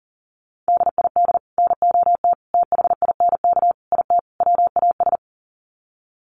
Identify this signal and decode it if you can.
Morse code, heard as a keyed tone, about 40 wpm, 700 Hz. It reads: DID NOT THINK IT WAS
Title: Sets of 5 Words Encore 40wpm